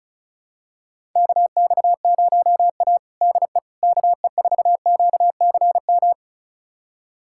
Morse code, heard as a keyed tone, about 34 wpm, 700 Hz. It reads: KX0A DE KE4QCM
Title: USA Callsign DE Callsign 35wpm